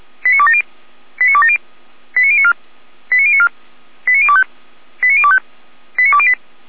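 Five-tone selective call (ZVEI), heard as quick runs of five tones, tones of about 80 ms. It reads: 89270 89270 89094 89094 89025 89025 89208
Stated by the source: ZVEI five-tone selective call 89358 received on SDR, from feuerwehr-orlamuende (Orlamünde fire brigade) audio server.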